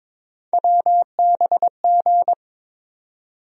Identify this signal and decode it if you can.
Morse code, heard as a keyed tone, about 22 wpm, 700 Hz. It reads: WBG
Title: USA Call Sign Postfixes - 3 Letters 22wpm